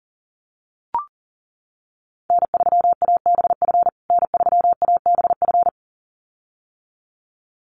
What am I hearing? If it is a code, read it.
Morse code, heard as a keyed tone, about 40 wpm, 700 Hz. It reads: N3ABF N3ABF
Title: USA Call Signs Repeated 2x 40wpm